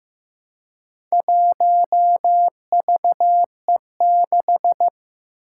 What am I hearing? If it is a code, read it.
Morse code, heard as a keyed tone, about 15 wpm, 700 Hz. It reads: 1VE6
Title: Four Challenging Letter-Number Combinations 15wpm